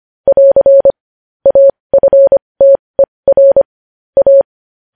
Morse code, heard as a keyed tone, about 25 wpm, spaced as for 20 wpm, 550 Hz. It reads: <AR> AFTER A
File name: podcast / News headlines-1x-25x20